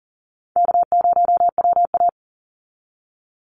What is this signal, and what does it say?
Morse code, heard as a keyed tone, about 40 wpm, 700 Hz. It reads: K0WA
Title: Easy Call Signs Repeated 2x - 40wpm